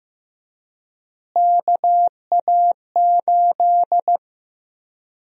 Morse code, heard as a keyed tone, about 15 wpm, 700 Hz. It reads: KA8
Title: USA Call Sign Prefixes - Repeated 2x - 15wpm